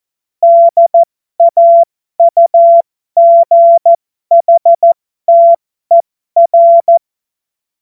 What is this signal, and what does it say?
Morse code, heard as a keyed tone, about 14 wpm, spaced as for 12 wpm, 680 Hz. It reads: DAUGHTER